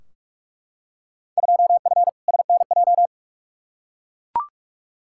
Morse code, heard as a keyed tone, about 42 wpm, 700 Hz. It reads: 2F SNJ
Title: Field Day Exchanges Encore - 45wpm